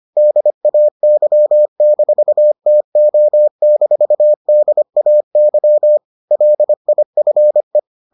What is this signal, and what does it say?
Morse code, heard as a keyed tone, about 25 wpm, 600 Hz. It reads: DAY-TO-DAY LIFE